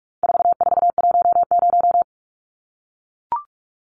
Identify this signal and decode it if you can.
Morse code, heard as a keyed tone, about 44 wpm, 700 Hz. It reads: 4410
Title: Numbers - 4-Digit 45wpm